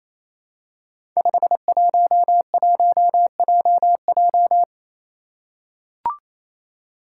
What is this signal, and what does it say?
Morse code, heard as a keyed tone, about 27 wpm, 700 Hz. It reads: 511JJ
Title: Five Challenging Letter-Number Combinations 28wpm